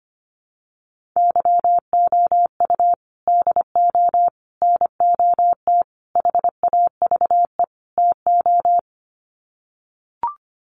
Morse code, heard as a keyed tone, about 25 wpm, 700 Hz. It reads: YOU DO NOT HAVE TO
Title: Sets of 5 Words 25wpm